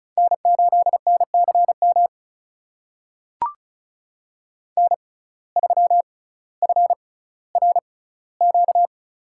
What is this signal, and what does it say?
Morse code, heard as a keyed tone, about 36 wpm, spaced as for 18 wpm, 700 Hz. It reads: N8NCM N3FRQ
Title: USA Call Signs - 5 Characters - 6x Character Spacing - 35wpm